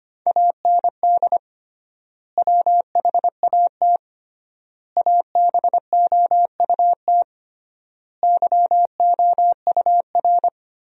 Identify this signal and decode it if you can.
Morse code, heard as a keyed tone, about 25 wpm, 700 Hz. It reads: AND WHAT ABOUT YOUR
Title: 3x Word-Spacing - Sentences from Top 200 Words - 25wpm